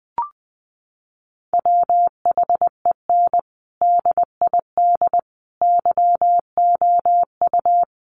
Morse code, heard as a keyed tone, about 20 wpm, 700 Hz. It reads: WHEN DID YOU
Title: Sentences from Top 1400 Words 20wpm